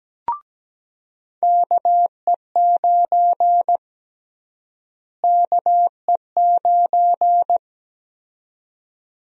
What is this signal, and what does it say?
Morse code, heard as a keyed tone, about 17 wpm, 700 Hz. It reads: KE9 KE9
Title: USA Call Sign Prefixes - Repeated 2x - 17wpm